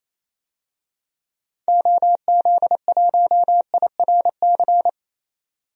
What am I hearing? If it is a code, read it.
Morse code, heard as a keyed tone, about 28 wpm, 700 Hz. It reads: OZ1IRC
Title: DXCC-100 Call Signs - 6 Characters - 28wpm